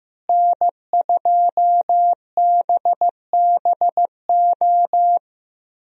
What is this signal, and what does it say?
Morse code, heard as a keyed tone, about 15 wpm, 700 Hz. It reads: N2BBO
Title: USA Call Signs - 5 Characters - Repeated 2x - 15wpm